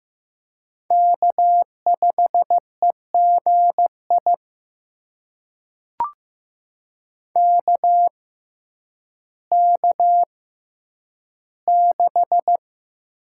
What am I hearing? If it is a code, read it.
Morse code, heard as a keyed tone, about 15 wpm, spaced as for 9 wpm, 700 Hz. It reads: K5EGI KK6
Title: USA Call Signs - 5 Characters - 6x Character Spacing - 15wpm